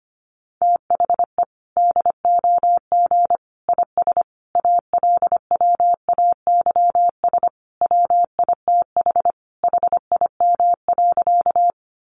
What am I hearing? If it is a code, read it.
Morse code, heard as a keyed tone, about 25 wpm, 700 Hz. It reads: THE DOG IS ALWAYS WITH HIM.
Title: Sentences from Top 400 Words 25wpm